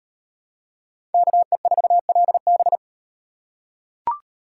Morse code, heard as a keyed tone, about 38 wpm, 700 Hz. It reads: KE4LB